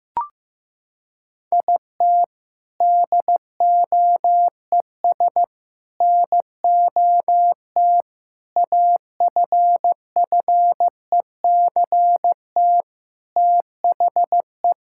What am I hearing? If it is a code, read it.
Morse code, heard as a keyed tone, about 15 wpm, 700 Hz. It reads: IT DOES NOT AFFECT THE
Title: Sentences from Top 1000 Words - Review Part 6 - 15wpm